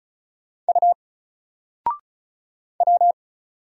Morse code, heard as a keyed tone, about 36 wpm, 700 Hz. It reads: U W
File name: Single-Letter-Number-35wpm